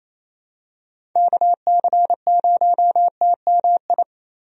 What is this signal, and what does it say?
Morse code, heard as a keyed tone, about 28 wpm, 700 Hz. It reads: KC0TMI
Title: USA Call Signs - 6 Characters - 28wpm